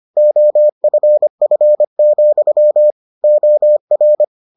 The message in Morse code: OFF, OR

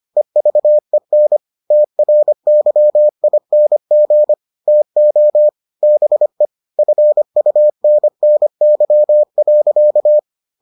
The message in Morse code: EVEN TRYING TO BE FUNNY.

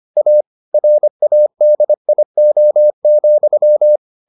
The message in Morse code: A RADIO,

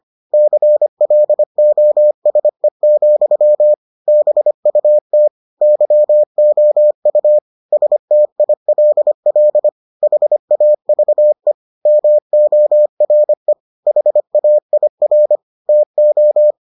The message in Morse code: CLOSE, BUT YOU STILL HAVE MORE HAIR TO

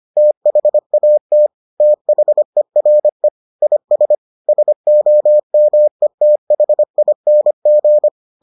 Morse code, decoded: THAT THERE IS SOMETHING

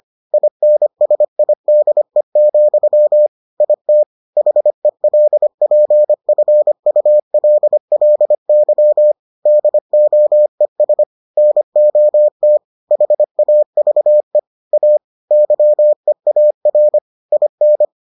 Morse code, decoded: INSIDE, IT HELPFULLY DOES NOT HAVE A YEAR IN

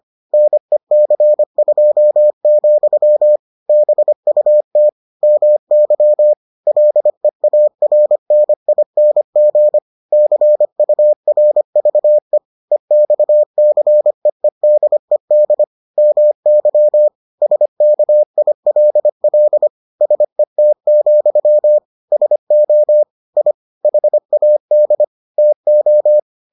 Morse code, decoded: NEC2, BUT MY LEARNING CURVE EXCEEDED MY SKILL SET, SO I HAD TO

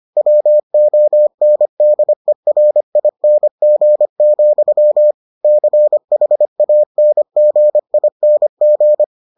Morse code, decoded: WONDERING, CHANGING

